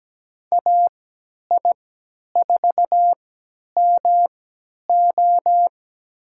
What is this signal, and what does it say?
Morse code, heard as a keyed tone, about 17 wpm, spaced as for 10 wpm, 700 Hz. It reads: AI4MO
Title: USA Call Signs - 5 Characters - 3x Character Spacing - 17wpm